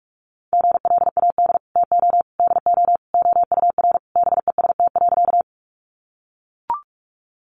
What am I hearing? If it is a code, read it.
Morse code, heard as a keyed tone, about 44 wpm, 700 Hz. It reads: GLAD TO DO OUR BEST.
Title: Sentences from Top 800 Words 45wpm